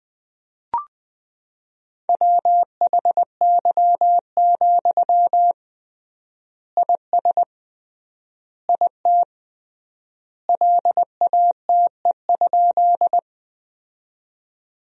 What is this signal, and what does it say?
Morse code, heard as a keyed tone, about 21 wpm, 700 Hz. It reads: WHY, IS IT LATE?
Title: 3x Word-Spacing - Sentences from Top 300 Words - 20wpm